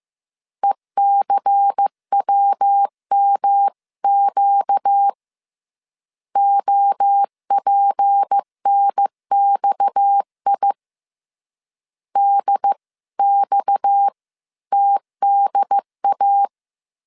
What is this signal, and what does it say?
Morse code, heard as a keyed tone, about 15 wpm, 800 Hz. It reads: ECWMQ OPNXI DX TDA
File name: CW / DET / 1 6 1500C S C 14wpm 2951